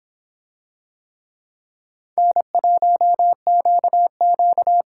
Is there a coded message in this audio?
Yes: Morse code from 2.2 to 4.8 s, heard as a keyed tone, about 25 wpm, 700 Hz. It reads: N1QQ